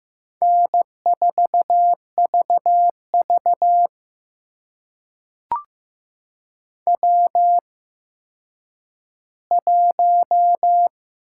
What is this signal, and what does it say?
Morse code, heard as a keyed tone, about 15 wpm, spaced as for 9 wpm, 700 Hz. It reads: N4VV W1
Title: USA Call Signs - 4 Characters - 8x Character Spacing - 15wpm